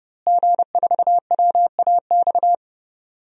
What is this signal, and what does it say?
Morse code, heard as a keyed tone, about 32 wpm, 700 Hz. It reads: G4WAX
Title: DXCC-100 Call Signs - 5 Characters - Repeated 3x - 30wpm